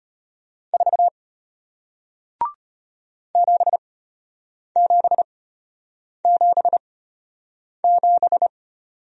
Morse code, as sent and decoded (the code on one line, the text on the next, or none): ....- / --... --... --... --...
4 7777